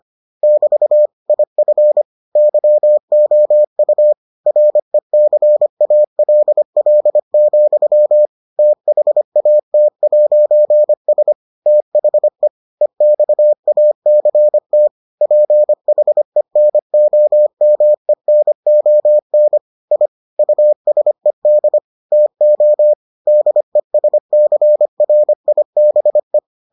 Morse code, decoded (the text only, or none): <BT> IF YOU RECALL, THAT'S THE EXACT PHENOMENON I USED TO DESCRIBE